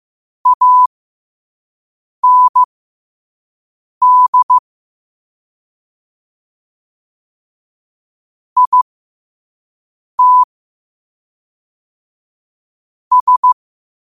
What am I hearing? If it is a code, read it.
Morse code, heard as a keyed tone, about 15 wpm, spaced as for 5 wpm, 1000 Hz. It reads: AND IT S
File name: qotdfiles / 5wpm